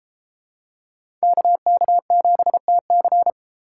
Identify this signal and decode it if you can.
Morse code, heard as a keyed tone, about 32 wpm, 700 Hz. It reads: KK7TC